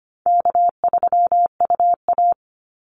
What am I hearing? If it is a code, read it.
Morse code, heard as a keyed tone, about 25 wpm, 700 Hz. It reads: K3UA
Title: Easy Call Signs - 25wpm